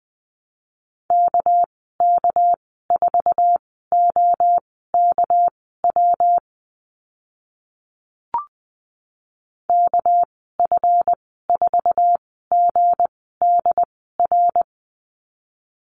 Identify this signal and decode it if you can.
Morse code, heard as a keyed tone, about 20 wpm, spaced as for 14 wpm, 700 Hz. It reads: KK4OKW KF4GDR
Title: 231 Lesson 37g - Letter x - Call Signs 20wpm